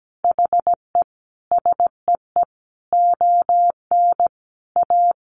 Morse code, heard as a keyed tone, about 17 wpm, 700 Hz. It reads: HE SEE ON A
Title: Sentences from Top 1300 Words 17wpm